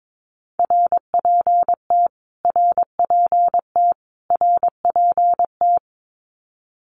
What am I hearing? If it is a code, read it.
Morse code, heard as a keyed tone, about 22 wpm, 700 Hz. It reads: RPT RPT RPT
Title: Abbreviations Repeated 3x - 22wpm